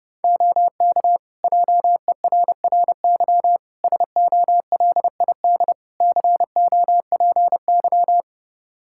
Morse code, heard as a keyed tone, about 30 wpm, 700 Hz. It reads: OK JERRY SOLID COPY
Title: Practice QSO - Exchange 3 - More Info - 30wpm